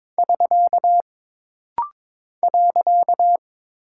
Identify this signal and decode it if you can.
Morse code, heard as a keyed tone, about 22 wpm, 700 Hz. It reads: <SK> .
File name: practice / 22wpm / Single Punctuation-Prosign - Rapid-Fire - 22wpm